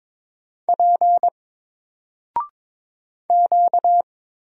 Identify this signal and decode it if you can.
Morse code, heard as a keyed tone, about 22 wpm, 700 Hz. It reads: P Q
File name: Single-Letter-Number-22wpm